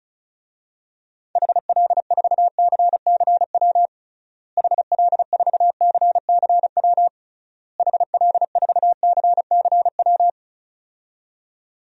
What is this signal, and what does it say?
Morse code, heard as a keyed tone, about 34 wpm, 700 Hz. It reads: HL4CCW HL4CCW HL4CCW
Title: DXCC-100 Call Signs - 6 Characters - Repeated 3x - 35wpm